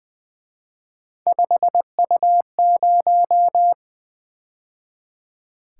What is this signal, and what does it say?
Morse code, heard as a keyed tone, about 23 wpm, 700 Hz. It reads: 5U0
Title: Speed-Racing - Three Letter-Number Combinations - 20wpm